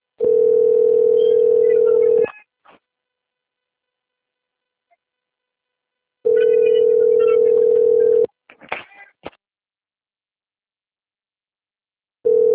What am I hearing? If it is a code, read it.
Ringback tone (North America), 440 + 480 Hz.
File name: rg-1600-unknown- - - .33323